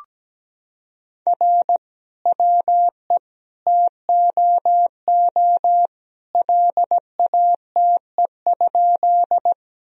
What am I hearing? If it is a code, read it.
Morse code, heard as a keyed tone, about 17 wpm, 700 Hz. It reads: R WE TOO LATE?
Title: Sentences from Top 300 Words and Abbreviations 17wpm